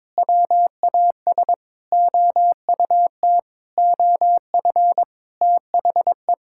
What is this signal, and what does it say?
Morse code, heard as a keyed tone, about 22 wpm, 700 Hz. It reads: WAS OUT OF THE
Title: Sentences from Top 1000 Words - Review Part 5 - 22wpm